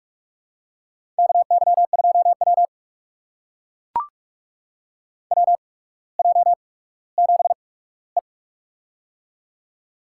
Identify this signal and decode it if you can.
Morse code, heard as a keyed tone, about 44 wpm, spaced as for 19 wpm, 700 Hz. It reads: KY2W WJ7E
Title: USA Call Signs - 4 Characters - 8x Character Spacing - 45wpm